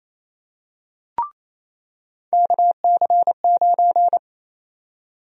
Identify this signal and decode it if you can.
Morse code, heard as a keyed tone, about 28 wpm, 700 Hz. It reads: KC9
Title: USA Call Sign Prefixes - 28wpm